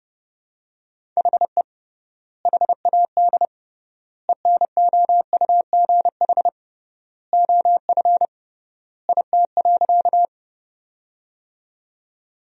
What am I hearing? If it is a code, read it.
Morse code, heard as a keyed tone, about 32 wpm, 700 Hz. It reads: HE HAD ENOUGH OF IT.
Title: 3x Word-Spacing - Sentences from Top 400 Words - 30wpm